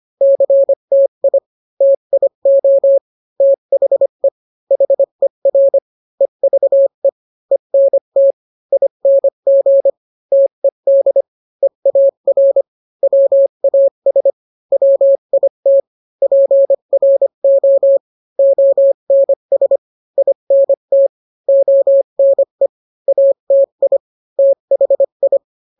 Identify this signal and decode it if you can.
Morse code, heard as a keyed tone, about 25 wpm, 550 Hz. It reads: CTI TIO THE HER EVE ENT ING TED EAR WAS WIT PRO ONS INT ONE ATI THI